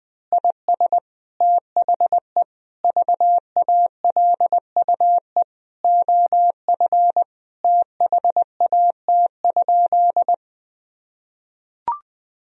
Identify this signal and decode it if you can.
Morse code, heard as a keyed tone, about 20 wpm, 700 Hz. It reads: IS THE VALUE OF THAT?
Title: Sentences from Top 1000 Words - Review Part 3 - 20wpm